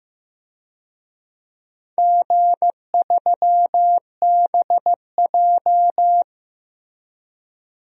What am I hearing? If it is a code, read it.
Morse code, heard as a keyed tone, about 15 wpm, 700 Hz. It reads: G3BJ
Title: DXCC-100 Call Signs - 4 Characters - Repeated 2x - 15wpm